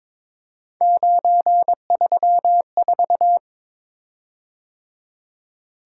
Morse code, heard as a keyed tone, about 22 wpm, 700 Hz. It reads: 934
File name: practice / 22wpm / Three Letter-Number Combinations - 22wpm